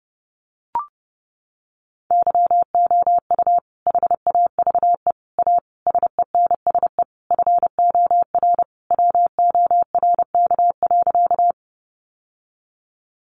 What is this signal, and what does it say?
Morse code, heard as a keyed tone, about 30 wpm, 700 Hz. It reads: YOU HAVE A SENSE FOR WORK.
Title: Sentences from Top 700 Words 30wpm